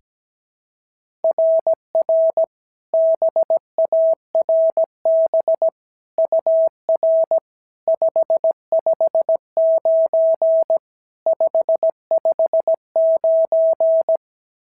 Morse code, heard as a keyed tone, about 16 wpm, 650 Hz. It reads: RR BARB UR 559 559
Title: SOTA QSO Practice 17wpm